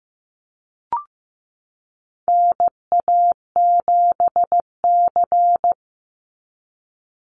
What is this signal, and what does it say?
Morse code, heard as a keyed tone, about 14 wpm, 700 Hz. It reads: NA7C
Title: Call Signs Repeated 2x - 15wpm